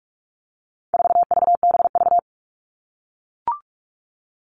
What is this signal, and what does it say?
Morse code, heard as a keyed tone, about 44 wpm, 700 Hz. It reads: 4VBV
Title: All Cut Numbers - 4-Digit 45wpm